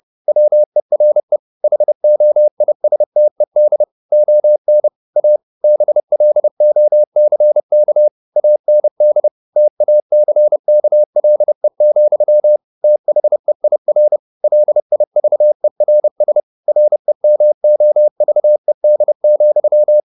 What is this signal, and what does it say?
Morse code, heard as a keyed tone, about 30 wpm, 600 Hz. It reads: WERE HOISTED ON A BLOCK AND TACKLE, THEIR LIVERS REMOVED,
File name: the old man and the sea / the old man and the sea 1 30wpm